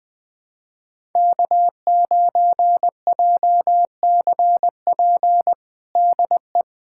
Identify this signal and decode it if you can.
Morse code, heard as a keyed tone, about 21 wpm, 700 Hz. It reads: K9JCP DE
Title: USA Callsign DE Callsign 20wpm